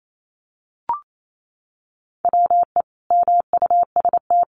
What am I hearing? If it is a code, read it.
Morse code, heard as a keyed tone, about 28 wpm, 700 Hz. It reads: WE MUST
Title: Sentences from Top 1400 Words 28wpm